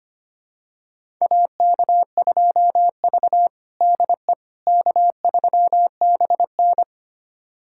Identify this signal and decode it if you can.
Morse code, heard as a keyed tone, about 25 wpm, 700 Hz. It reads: AK2V DE K3BN
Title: USA Callsign DE Callsign - 4 Characters - 25wpm